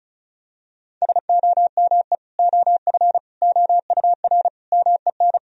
Morse code, heard as a keyed tone, about 34 wpm, 700 Hz. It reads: SOME OF OUR MEN
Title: Sentences from Top 1000 Words - Review Part 2 - 35wpm